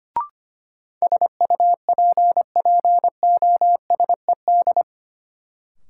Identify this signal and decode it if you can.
Morse code, heard as a keyed tone, about 25 wpm, 700 Hz. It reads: SUPPOSED
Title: Sets of 3 Words - New Word Drill - 25wpm